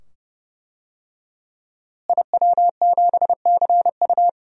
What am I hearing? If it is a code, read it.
Morse code, heard as a keyed tone, about 32 wpm, 700 Hz. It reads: IW7CU
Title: DXCC-100 Call Signs - 5 Characters - Repeated 2x - 30wpm